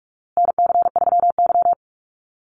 Morse code, heard as a keyed tone, about 44 wpm, 700 Hz. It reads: NC3Y
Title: Call Signs Repeated 2x - 45wpm